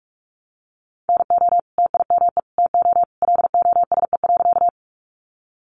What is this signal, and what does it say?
Morse code, heard as a keyed tone, about 44 wpm, 700 Hz. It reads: NO TIME TO LOSE.
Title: Sentences from Top 1100 Words 45wpm